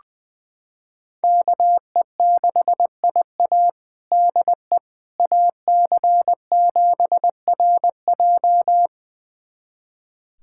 Morse code, heard as a keyed tone, about 21 wpm, 700 Hz. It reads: KE6IA DE AC7RJ
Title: USA Callsign DE Callsign - 5 Characters - 20wpm